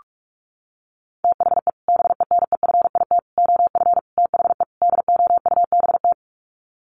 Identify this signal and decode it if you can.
Morse code, heard as a keyed tone, about 44 wpm, 700 Hz. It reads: THE BENEFIT OF THE DOUBT
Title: Sets of 5 Words Encore 45wpm